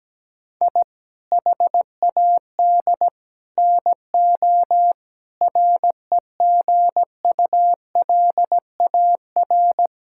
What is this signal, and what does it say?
Morse code, heard as a keyed tone, about 17 wpm, 700 Hz. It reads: I HAD NO REGULAR
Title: Sentences from Top 1000 Words - Review Part 6 - 17wpm